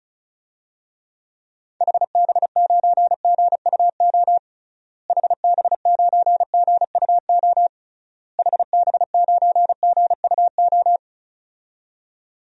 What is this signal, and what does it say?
Morse code, heard as a keyed tone, about 34 wpm, 700 Hz. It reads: HB9GUO HB9GUO HB9GUO
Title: DXCC-100 Call Signs - 6 Characters - Repeated 3x - 35wpm